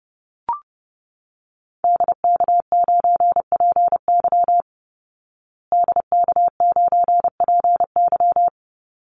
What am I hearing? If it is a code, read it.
Morse code, heard as a keyed tone, about 30 wpm, 700 Hz. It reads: DK9PY DK9PY
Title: Call Signs Repeated 2x - 30wpm